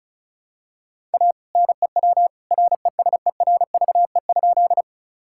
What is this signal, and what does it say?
Morse code, heard as a keyed tone, about 36 wpm, 700 Hz. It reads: A NEW RESERVE?
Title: Sentences from Top 1000 Words - Review Part 6 - 35wpm